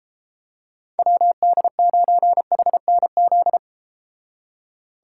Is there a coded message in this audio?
Yes: Morse code from 1.0 to 3.6 s, heard as a keyed tone, about 34 wpm, 700 Hz. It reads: WD9HNZ